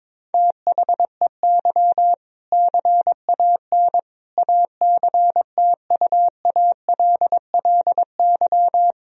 Morse code, heard as a keyed tone, about 22 wpm, 700 Hz. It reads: THEY CAN ACTUALLY
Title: Sentences from Top 1000 Words - Review Part 4 - 22wpm